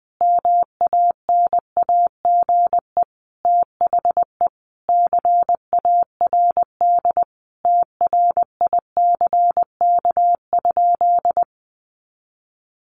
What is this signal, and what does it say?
Morse code, heard as a keyed tone, about 20 wpm, 700 Hz. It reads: MANAGE THE CARD TRICK?
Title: Sentences from Top 1500 Words 20wpm